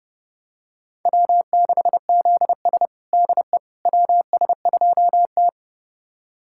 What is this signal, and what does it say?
Morse code, heard as a keyed tone, about 30 wpm, 700 Hz. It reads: W6ZS DE WS2T
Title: USA Callsign DE Callsign - 4 Characters - 30wpm